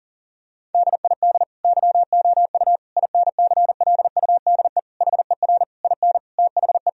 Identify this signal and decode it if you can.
Morse code, heard as a keyed tone, about 40 wpm, 700 Hz. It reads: DID YOU INCLUDE HER IN THE
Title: Sentences from Top 1000 Words - Review Part 3 - 40wpm